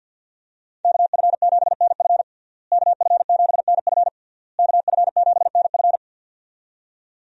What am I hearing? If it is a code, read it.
Morse code, heard as a keyed tone, about 52 wpm, 700 Hz. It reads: KF7NF KF7NF KF7NF
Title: USA Call Signs - 5 Characters - Repeated 3x - 50wpm